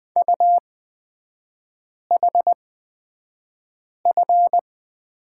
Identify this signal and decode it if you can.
Morse code, heard as a keyed tone, about 20 wpm, spaced as for 5 wpm, 700 Hz. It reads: UHF
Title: 255 Lesson 41e - Period - Three Characters 20wpm